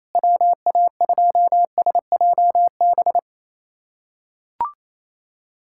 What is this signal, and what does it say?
Morse code, heard as a keyed tone, about 28 wpm, 700 Hz. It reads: WA2SJB
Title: USA Callsign DE Callsign 28wpm